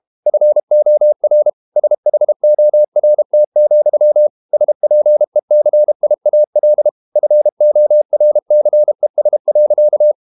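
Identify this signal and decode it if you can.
Morse code, heard as a keyed tone, about 32 wpm, 600 Hz. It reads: FOR SHORT, SPECIAL FORCES.
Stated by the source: Morse code practice at 32 wpm from CW med Gnister, reading Ken Follett's novel Never in English.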